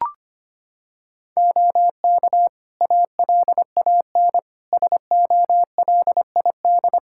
Morse code, heard as a keyed tone, about 25 wpm, 700 Hz. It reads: OK ALAN SOLID
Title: Practice QSO - Exchange 3 - More Info - 25wpm